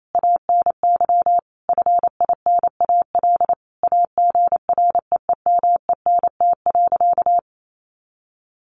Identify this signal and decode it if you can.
Morse code, heard as a keyed tone, about 28 wpm, 700 Hz. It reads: ANY FINAL AGREEMENT.
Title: Sentences from Top 1500 Words 28wpm